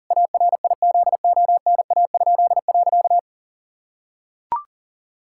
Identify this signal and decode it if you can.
Morse code, heard as a keyed tone, about 40 wpm, 700 Hz. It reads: ARIZONA?.
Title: Pangram Challenge v2 - 40wpm